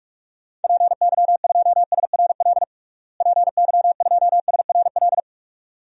Morse code, heard as a keyed tone, about 44 wpm, 700 Hz. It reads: PY2SRL PY2SRL
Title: DXCC-100 Call Signs - 6 Characters - Repeated 2x - 45wpm